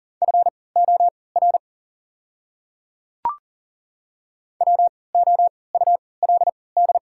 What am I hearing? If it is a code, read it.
Morse code, heard as a keyed tone, about 40 wpm, spaced as for 23 wpm, 690 Hz. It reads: FOR WOULD
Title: 3x Character Spacing - Top 100 Words - 40wpm